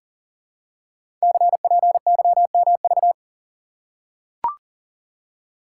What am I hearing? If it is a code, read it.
Morse code, heard as a keyed tone, about 42 wpm, 700 Hz. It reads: CPYMV
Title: Code Groups - 5 Letter:Numbers - 40wpm